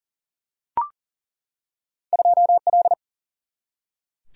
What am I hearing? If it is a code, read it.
Morse code, heard as a keyed tone, about 40 wpm, 700 Hz. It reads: 2L
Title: Single-Letter Single-Number 40wpm